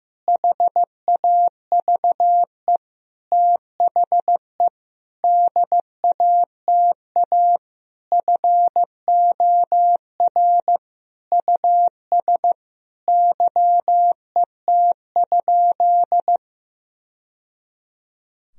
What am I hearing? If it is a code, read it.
Morse code, heard as a keyed tone, about 15 wpm, 700 Hz. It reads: HAVE THE DATA FOR US YET?
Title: Sentences from Top 1000 Words - Review Part 3 - 15wpm